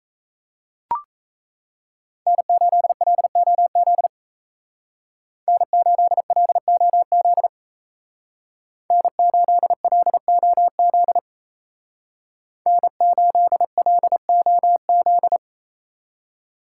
Morse code, heard as a keyed tone, about 35 wpm, 700 Hz. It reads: N8LOZ N8LOZ N8LOZ N8LOZ
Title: Speed Racing - USA 1x3 Call Signs 28wpm